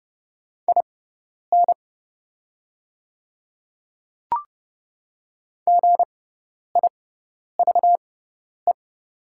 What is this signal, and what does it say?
Morse code, heard as a keyed tone, about 30 wpm, spaced as for 10 wpm, 700 Hz. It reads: IN GIVE
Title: 6x Character Spacing - Top 200 Words - 30wpm